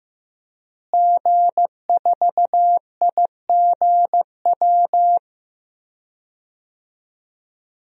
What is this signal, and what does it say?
Morse code, heard as a keyed tone, about 15 wpm, 700 Hz. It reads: G4IGW
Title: DXCC-100 Call Signs - 5 Characters - Repeated 2x - 15wpm